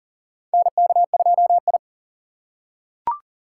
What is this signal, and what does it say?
Morse code, heard as a keyed tone, about 40 wpm, 700 Hz. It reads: NK2I
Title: USA Call Signs - 2x Character Spacing - 40wpm